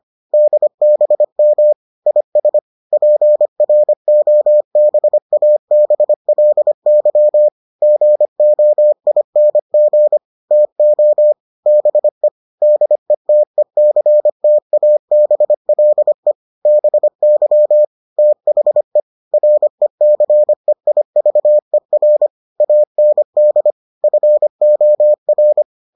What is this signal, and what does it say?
Morse code, heard as a keyed tone, about 25 wpm, 600 Hz. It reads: DBM IS PROBABLY GOING TO BE DETECTABLE BY THE RECEIVER AND FOR